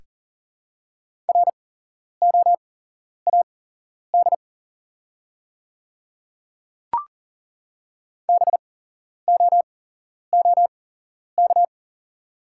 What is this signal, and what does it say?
Morse code, heard as a keyed tone, about 40 wpm, spaced as for 11 wpm, 700 Hz. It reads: ROAD BOOK
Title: 8x Character Spacing - Top 500 Words - 40wpm